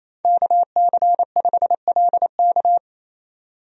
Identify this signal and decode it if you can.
Morse code, heard as a keyed tone, about 27 wpm, 700 Hz. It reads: KC5LK
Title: USA Call Signs - 5 Characters - 28wpm